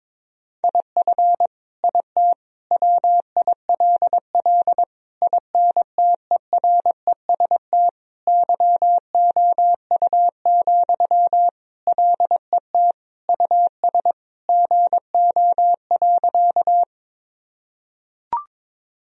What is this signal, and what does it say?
Morse code, heard as a keyed tone, about 22 wpm, 700 Hz. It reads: IF IT WILL INTEREST YOU, LET US GO.
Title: Sentences from Top 1000 Words - Review Part 2 - 22wpm